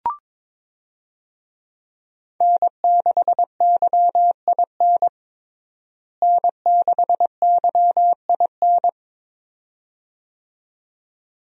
Morse code, heard as a keyed tone, about 22 wpm, 700 Hz. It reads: N6YIN N6YIN
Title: USA Call Signs - 5 Characters - Repeated 2x - 22wpm